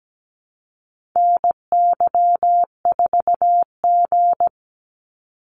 Morse code, heard as a keyed tone, about 17 wpm, 700 Hz. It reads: NY4G
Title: NA SOTA Activator Callsigns 17wpm